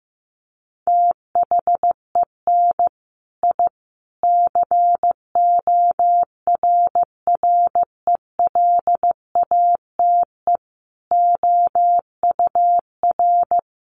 Morse code, heard as a keyed tone, about 15 wpm, 700 Hz. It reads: THEN I CORRELATE OUR
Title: A Sound Of Thunder 15wpm